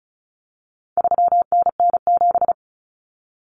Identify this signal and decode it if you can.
Morse code, heard as a keyed tone, about 34 wpm, 700 Hz. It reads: 3NN7
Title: Common Cut Numbers - 4-Digit 35wpm